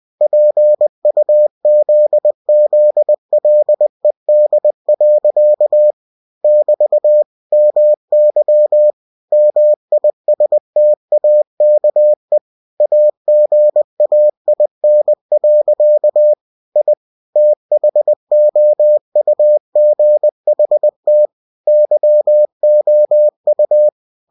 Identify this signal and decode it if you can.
Morse code, heard as a keyed tone, about 20 wpm, 600 Hz. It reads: PUZZLED. <BT> MY MISTAKE AGAIN. I THOUGHT YOU
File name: cw-books / winnie-the-pooh / winnie-20wpm-page-0089